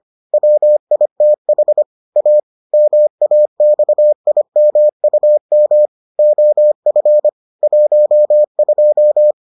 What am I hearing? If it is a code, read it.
Morse code, heard as a keyed tone, about 25 wpm, 600 Hz. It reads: WITH A MAXIMUM OF 12